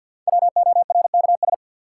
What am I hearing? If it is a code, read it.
Morse code, heard as a keyed tone, about 52 wpm, 700 Hz. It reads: WORKS
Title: Sentences from Top 1000 Words - Review Part 5 - New Word Drill - 50wpm